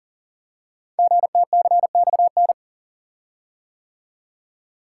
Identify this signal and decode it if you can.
Morse code, heard as a keyed tone, about 39 wpm, 700 Hz. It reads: GTCXN
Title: Code Groups - 5 Letters - 40wpm